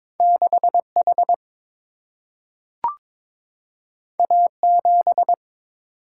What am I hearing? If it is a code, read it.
Morse code, heard as a keyed tone, about 22 wpm, 700 Hz. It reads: 6 H A 7
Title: Single-Letter Single-Number 22wpm